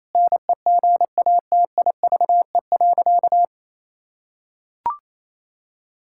Morse code, heard as a keyed tone, about 28 wpm, 700 Hz. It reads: NEGATIVE.
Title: Sentences from Top 1000 Words - Review Part 6 - 28wpm